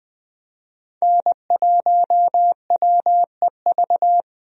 Morse code, heard as a keyed tone, about 20 wpm, 700 Hz. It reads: N1WEV
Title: USA Call Signs - 5 Characters - 20wpm